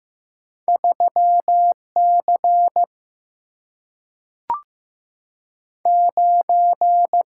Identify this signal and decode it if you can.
Morse code, heard as a keyed tone, about 15 wpm, 700 Hz. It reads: 3 C 9
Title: Single-Letter Single-Number 15wpm